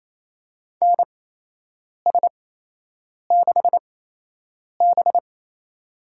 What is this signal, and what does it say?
Morse code, heard as a keyed tone, about 28 wpm, spaced as for 8 wpm, 700 Hz. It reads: NS6B
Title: USA Call Signs - 4 Characters - 8x Character Spacing - 28wpm